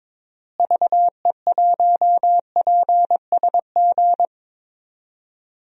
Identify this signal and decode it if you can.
Morse code, heard as a keyed tone, about 23 wpm, 700 Hz. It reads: VE1PSG
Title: DXCC-100 Call Signs - 6 Characters - Repeated 3x - 22wpm